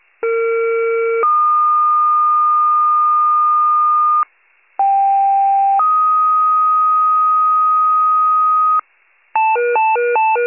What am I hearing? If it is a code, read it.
Two-tone page, heard as a one-second tone, then a three-second tone, about 460 then 1220 Hz, about 790 then 1250 Hz. It is heard from 0.2 to 8.8 s.
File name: pager / 21 RESCUE / wcalert